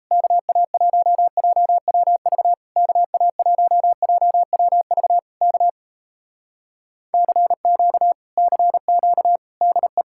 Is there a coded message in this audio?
Yes: Morse code from 0.1 to 10.0 s, heard as a keyed tone, about 37 wpm, 700 Hz. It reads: KA1JWV KA1JWV K CQ CQ DE